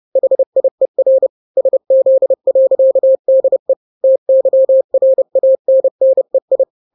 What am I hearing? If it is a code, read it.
Morse code, heard as a keyed tone, about 30 wpm, 530 Hz. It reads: HIER SZ.DE TYRANNEI
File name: daily / 29wpm